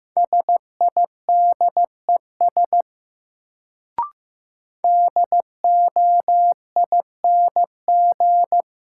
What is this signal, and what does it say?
Morse code, heard as a keyed tone, about 15 wpm, 700 Hz. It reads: SIDES DOING
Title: Sets of 4 Words - New Word Drill - 15wpm